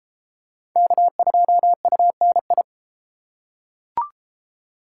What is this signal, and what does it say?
Morse code, heard as a keyed tone, about 30 wpm, 700 Hz. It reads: K2UNI